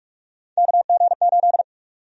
Morse code, heard as a keyed tone, about 44 wpm, 700 Hz. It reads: KG8
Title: USA Call Sign Prefixes - Repeated 2x - 45wpm